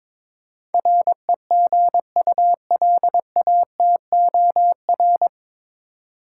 Morse code, heard as a keyed tone, about 22 wpm, 700 Hz. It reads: REGULATOR
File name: 9 Letter Words - First 3 Letters Given - 22wpm